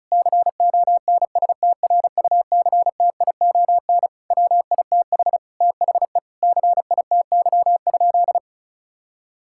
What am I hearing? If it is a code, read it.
Morse code, heard as a keyed tone, about 34 wpm, 700 Hz. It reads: CONSTRUCTION WITH THE CITY?
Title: Sentences from Top 1600 Words 35wpm